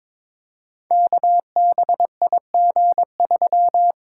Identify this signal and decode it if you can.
Morse code, heard as a keyed tone, about 21 wpm, 700 Hz. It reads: KBIG3
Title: Code Groups - 5 Letter:Numbers - 22wpm